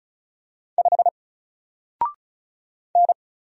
Morse code, heard as a keyed tone, about 36 wpm, 700 Hz. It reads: 5 N
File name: Single-Letter-Number-35wpm